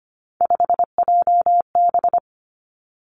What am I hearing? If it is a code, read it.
Morse code, heard as a keyed tone, about 25 wpm, 700 Hz. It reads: 5JB
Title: Three Challenging Letter-Number Combinations 25wpm